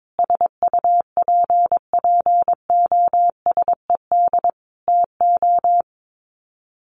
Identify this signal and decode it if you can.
Morse code, heard as a keyed tone, about 22 wpm, 700 Hz. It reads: SUPPOSED TO
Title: Sets of 5 Words Encore2 22wpm